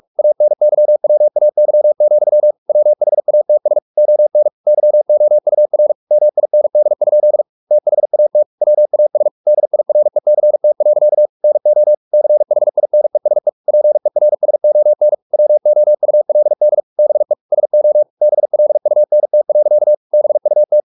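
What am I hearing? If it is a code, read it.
Morse code, heard as a keyed tone, about 44 wpm, 600 Hz. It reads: ANYWAY, WHATS ON YOUR MIND? THAT WAS DIRECT. NO CHINESE PERSON WOULD BE SO BLUNT. BUT
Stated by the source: Morse practice at 45 wpm, the novel Never by Ken Follett in English, from CW med Gnister.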